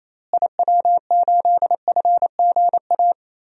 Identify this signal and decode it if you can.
Morse code, heard as a keyed tone, about 28 wpm, 700 Hz. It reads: IW8FGA
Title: DXCC-100 Call Signs - 6 Characters - Repeated 3x - 28wpm